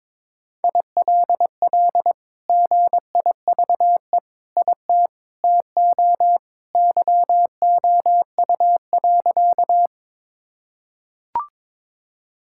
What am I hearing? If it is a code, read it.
Morse code, heard as a keyed tone, about 22 wpm, 700 Hz. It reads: ILL GIVE IT TO YOU.
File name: practice / 22wpm / Sentences with 2-Word Phrases - 22wpm